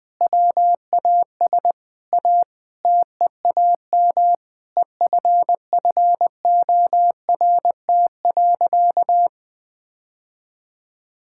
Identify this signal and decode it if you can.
Morse code, heard as a keyed tone, about 19 wpm, 700 Hz. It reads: WAS A TEAM EFFORT.
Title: Sentences with 8-Word Phrases - 20wpm